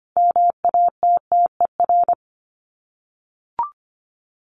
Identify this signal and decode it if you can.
Morse code, heard as a keyed tone, about 25 wpm, 700 Hz. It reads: MATTER
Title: Sets of 5 Words 25wpm